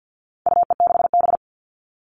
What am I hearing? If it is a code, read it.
Morse code, heard as a keyed tone, about 50 wpm, 700 Hz. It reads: UE6B
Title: All Cut Numbers - 4-Digit 50wpm